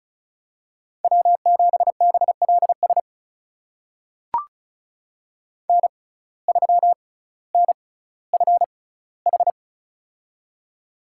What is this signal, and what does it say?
Morse code, heard as a keyed tone, about 36 wpm, spaced as for 18 wpm, 700 Hz. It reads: W7BLS N3NFH
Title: USA Call Signs - 5 Characters - 6x Character Spacing - 35wpm